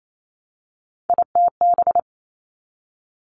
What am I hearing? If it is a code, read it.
Morse code, heard as a keyed tone, about 28 wpm, 700 Hz. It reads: ITB
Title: Three Challenging Letter-Number Combinations 28wpm